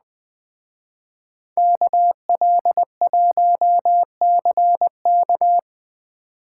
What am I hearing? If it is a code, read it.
Morse code, heard as a keyed tone, about 21 wpm, 700 Hz. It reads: KL1CK